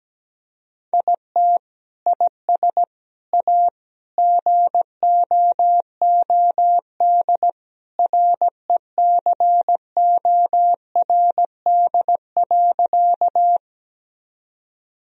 Morse code, heard as a keyed tone, about 17 wpm, 700 Hz. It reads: IT IS A GOOD RECORD.
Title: Sentences from Top 1000 Words - Review Part 3 - 17wpm